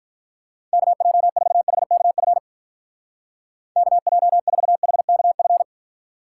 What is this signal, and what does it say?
Morse code, heard as a keyed tone, about 45 wpm, 700 Hz. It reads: KJ4HKF KJ4HKF